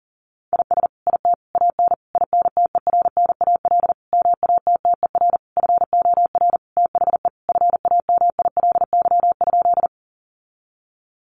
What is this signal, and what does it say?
Morse code, heard as a keyed tone, about 37 wpm, 700 Hz. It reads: IS IT AN INTERNAL MATTER FOR THE FAMILY?
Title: Sentences from Top 1400 Words 40wpm